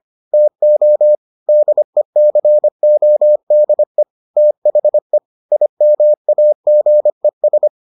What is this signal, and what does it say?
Morse code, heard as a keyed tone, about 25 wpm, 600 Hz. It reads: TO DECODE THE IMAGES